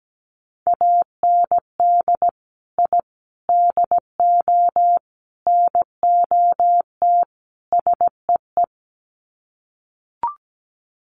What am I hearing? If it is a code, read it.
Morse code, heard as a keyed tone, about 17 wpm, 700 Hz. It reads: AND I DO NOT SEE
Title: Sets of 5 Words Encore2 17wpm